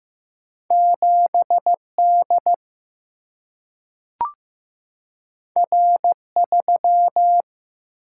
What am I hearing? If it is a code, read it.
Morse code, heard as a keyed tone, about 15 wpm, 700 Hz. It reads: 7 D R 3
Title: Single-Letter Single-Number 15wpm